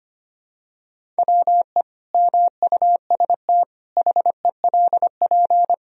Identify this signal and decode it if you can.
Morse code, heard as a keyed tone, about 25 wpm, 700 Hz. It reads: WE MUST HELP
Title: Sentences from Top 1000 Words - Review Part 1 - 25wpm